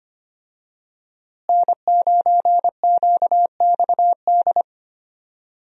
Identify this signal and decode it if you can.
Morse code, heard as a keyed tone, about 25 wpm, 700 Hz. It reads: N9QXD
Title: USA Call Signs - 5 Characters - Repeated 2x - 25wpm